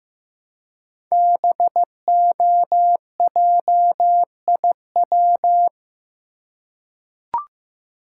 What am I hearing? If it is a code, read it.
Morse code, heard as a keyed tone, about 15 wpm, 700 Hz. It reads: BOJIW
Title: Code Groups - 5 Letters - 15wpm